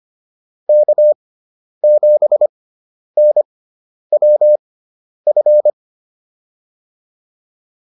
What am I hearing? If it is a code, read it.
Morse code, heard as a keyed tone, about 25 wpm, spaced as for 10 wpm, 600 Hz. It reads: K7NWF
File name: QSO209 10